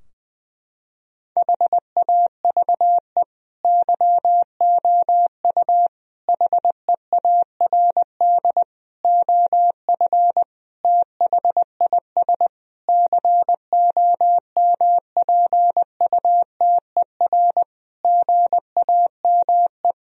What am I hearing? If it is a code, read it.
Morse code, heard as a keyed tone, about 19 wpm, 700 Hz. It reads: HAVE YOU HEARD OF THIS COMPUTER GAME
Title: Sentences from Top 1000 Words - Review Part 3 - 20wpm